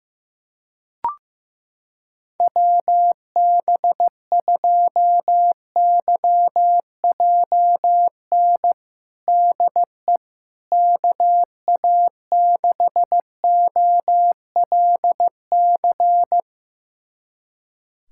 Morse code, heard as a keyed tone, about 15 wpm, 700 Hz. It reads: WB2YJN DE KA6OLC
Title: USA Callsign DE Callsign - 6 Characters - 15wpm